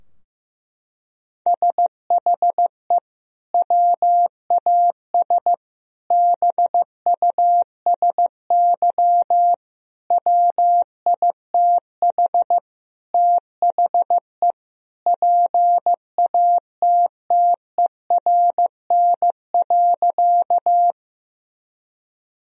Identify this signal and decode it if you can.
Morse code, heard as a keyed tone, about 15 wpm, 700 Hz. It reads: SHE WAS BUSY WITH THE PATTERN.
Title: Sentences from Top 1000 Words - Review Part 2 - 15wpm